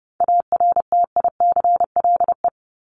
Morse code, heard as a keyed tone, about 30 wpm, 700 Hz. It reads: ARTICLE
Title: Top 751-1000 Words 30wpm